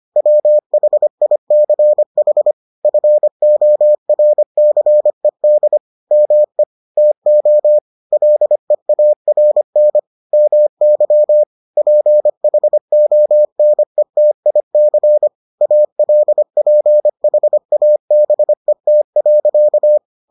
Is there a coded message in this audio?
Yes: Morse code from 0.2 to 20.0 s, heard as a keyed tone, about 25 wpm, 600 Hz. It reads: WHICH FORCED ME TO LEARN MY PHONETIC ALPHABET.